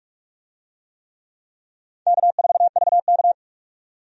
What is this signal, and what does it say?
Morse code, heard as a keyed tone, about 46 wpm, 700 Hz. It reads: K4VK
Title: USA Call Signs - 4 Characters - 45wpm